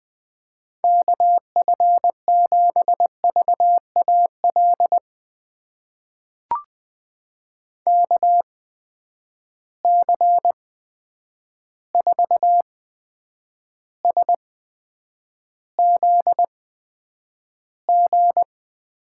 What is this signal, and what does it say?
Morse code, heard as a keyed tone, about 20 wpm, spaced as for 9 wpm, 700 Hz. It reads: KF7VAL KC4SZG
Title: USA Call Signs - 6 Characters - 8x Character Spacing - 20wpm